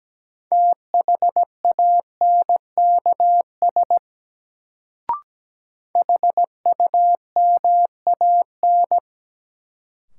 Morse code, heard as a keyed tone, about 17 wpm, 700 Hz. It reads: THANKS HUMAN
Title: Sets of 5 Words - New Word Drill - 17wpm